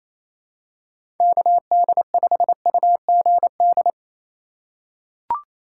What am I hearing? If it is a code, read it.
Morse code, heard as a keyed tone, about 28 wpm, 700 Hz. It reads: KD5UGD
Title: USA Call Signs - 6 Characters - 3x Character Spacing - 28wpm